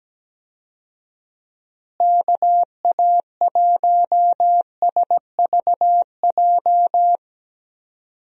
Morse code, heard as a keyed tone, about 17 wpm, 700 Hz. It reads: KA1SVJ
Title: USA Call Signs - 6 Characters - Repeated 2x - 17wpm